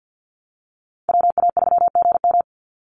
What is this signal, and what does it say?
Morse code, heard as a keyed tone, about 50 wpm, 700 Hz. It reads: WA3GM
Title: Call Signs - 50wpm